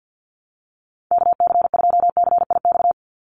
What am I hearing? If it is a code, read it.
Morse code, heard as a keyed tone, about 50 wpm, 700 Hz. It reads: KC2CIX
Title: USA Call Signs 50wpm